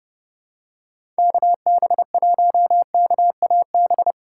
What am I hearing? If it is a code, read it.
Morse code, heard as a keyed tone, about 30 wpm, 700 Hz. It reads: KB1KAB
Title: USA Call Signs - 6 Characters - 2x Character Spacing - 30wpm